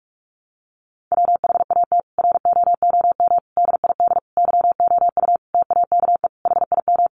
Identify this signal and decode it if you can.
Morse code, heard as a keyed tone, about 44 wpm, 700 Hz. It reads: WHAT ROOM DID YOU TAKE HIM
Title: Sentences from Top 600 Words 45wpm